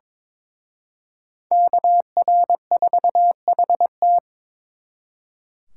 Morse code, heard as a keyed tone, about 26 wpm, 700 Hz. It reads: KR4HT
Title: Speed Racing - USA 2x2 Call Signs 22wpm